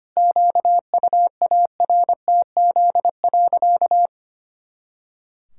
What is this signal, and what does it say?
Morse code, heard as a keyed tone, about 25 wpm, 700 Hz. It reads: QUARTZ.
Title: Pangram Challenge v2 - 25wpm